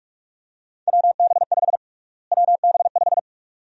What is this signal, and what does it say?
Morse code, heard as a keyed tone, about 46 wpm, 700 Hz. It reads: WB5 WB5
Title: USA Call Sign Prefixes - Repeated 2x - 45wpm